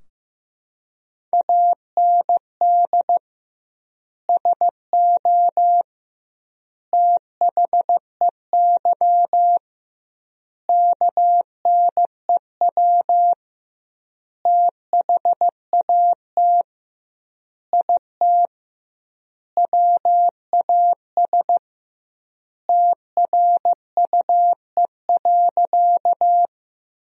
Morse code, heard as a keyed tone, about 15 wpm, 700 Hz. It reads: AND SO THEY KNEW THAT IT WAS TRUE.
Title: 2x Word-Spacing - Sentences from Top 400 Words - 15wpm